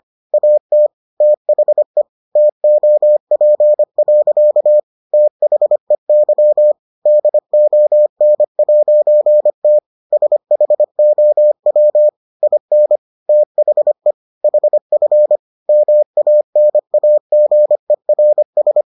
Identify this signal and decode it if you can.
Morse code, heard as a keyed tone, about 25 wpm, 600 Hz. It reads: AT THE TOP. THEY DON'T SHOW IN THE HF MANAGERS